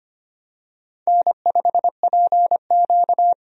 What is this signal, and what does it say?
Morse code, heard as a keyed tone, about 25 wpm, 700 Hz. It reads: N5PQ
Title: USA Call Signs - 4 Characters - 25wpm